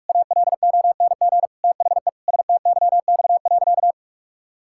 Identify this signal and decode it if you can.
Morse code, heard as a keyed tone, about 46 wpm, 700 Hz. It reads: ALONG THE STYX.
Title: Pangram Challenge v2 - 45wpm